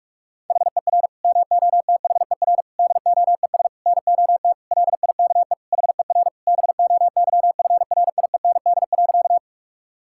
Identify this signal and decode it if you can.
Morse code, heard as a keyed tone, about 46 wpm, 700 Hz. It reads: HER MOTHER DOES NOT LIKE HER BOYFRIEND.
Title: Sentences from Top 1000 Words - Review Part 5 - 45wpm